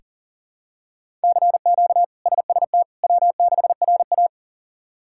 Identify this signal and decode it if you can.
Morse code, heard as a keyed tone, about 40 wpm, 700 Hz. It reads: CQ SST W6RA
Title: K1USN SST QSO Practice - 40wpm